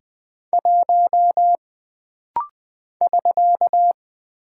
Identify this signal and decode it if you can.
Morse code, heard as a keyed tone, about 20 wpm, 700 Hz. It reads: 1 <SK>
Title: Single Character (Letter:Number:Punctuation:Prosign) - Rapid-Fire - 20wpm